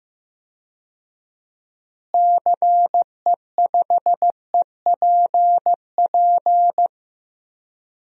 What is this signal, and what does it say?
Morse code, heard as a keyed tone, about 15 wpm, 700 Hz. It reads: CE5EPP
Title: DXCC-100 Call Signs - 6 Characters - Repeated 3x - 15wpm